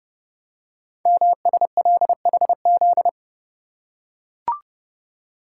Morse code, heard as a keyed tone, about 30 wpm, 700 Hz. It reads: MSLHZ
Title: Code Groups - 5 Letter:Numbers - 30wpm